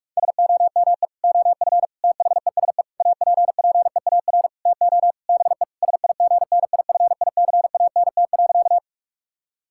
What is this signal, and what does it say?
Morse code, heard as a keyed tone, about 46 wpm, 700 Hz. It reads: SOME OF THESE APPEAR TO BE SIGNIFICANT.
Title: Sentences from Top 1000 Words - Review Part 6 - 45wpm